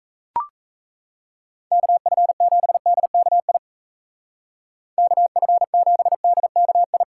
Morse code, heard as a keyed tone, about 41 wpm, 700 Hz. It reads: KF7DKI KF7DKI